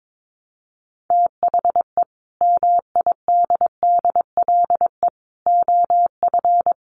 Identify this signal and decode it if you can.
Morse code, heard as a keyed tone, about 22 wpm, 700 Hz. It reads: THE MIDDLE OF
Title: Sets of 5 Words Encore2 22wpm